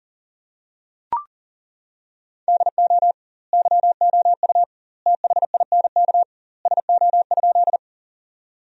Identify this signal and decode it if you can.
Morse code, heard as a keyed tone, about 40 wpm, 700 Hz. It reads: DO YOU THINK SO?
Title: 2x Word-Spacing - Sentences from Top 200 Words - 40wpm